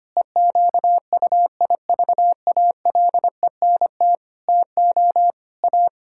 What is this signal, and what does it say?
Morse code, heard as a keyed tone, about 25 wpm, 700 Hz. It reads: EQUIVALENT TO A
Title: Sentences from Top 1600 Words 25wpm